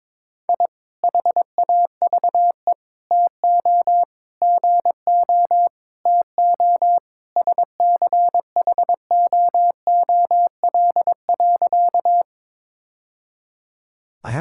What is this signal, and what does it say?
Morse code, heard as a keyed tone, about 21 wpm, 700 Hz. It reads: I HAVE TO GO TO SCHOOL.
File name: practice / 22wpm / Sentences with 3-Word Phrases - 22wpm